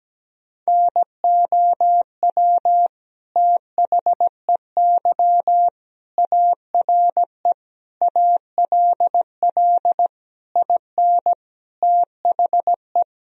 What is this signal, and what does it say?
Morse code, heard as a keyed tone, about 17 wpm, 700 Hz. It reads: NOW THEY ARE ALL IN THE
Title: Sentences from Top 1000 Words - Review Part 2 - 17wpm